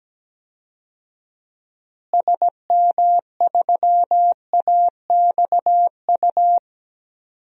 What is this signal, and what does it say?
Morse code, heard as a keyed tone, about 17 wpm, 700 Hz. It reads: SM3AXU
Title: DXCC-100 Call Signs - 6 Characters - Repeated 3x - 17wpm